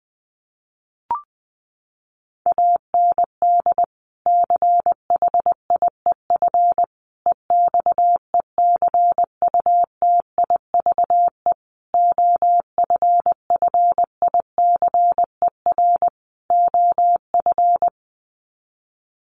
Morse code, heard as a keyed tone, about 20 wpm, 700 Hz. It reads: AND CHIEF EXECUTIVE OFFICER OF
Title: Sets of 5 Words Encore2 20wpm